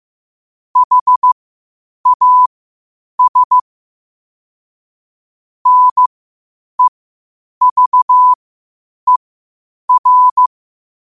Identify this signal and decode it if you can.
Morse code, heard as a keyed tone, about 15 wpm, spaced as for 8 wpm, 1000 Hz. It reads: HAS NEVER
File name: qotdfiles / 8wpm